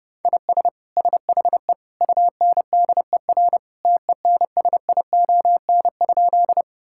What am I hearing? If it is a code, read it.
Morse code, heard as a keyed tone, about 30 wpm, 700 Hz. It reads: IS SHE UNDER TENSION?
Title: Sentences from Top 1600 Words 30wpm